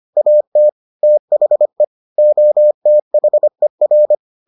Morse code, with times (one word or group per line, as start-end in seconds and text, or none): 0.2-0.7 AT
1.0-1.8 THE
2.2-4.1 OTHER